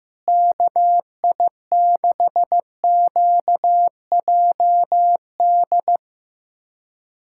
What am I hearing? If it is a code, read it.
Morse code, heard as a keyed tone, about 16 wpm, 700 Hz. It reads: KI6QJD
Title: USA Callsign DE Callsign - 6 Characters - 15wpm